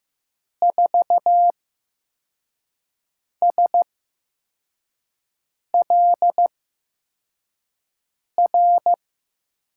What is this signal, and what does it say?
Morse code, heard as a keyed tone, about 15 wpm, spaced as for 4 wpm, 700 Hz. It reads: 4SLR
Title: USA Call Signs - 6 Characters - 8x Character Spacing - 15wpm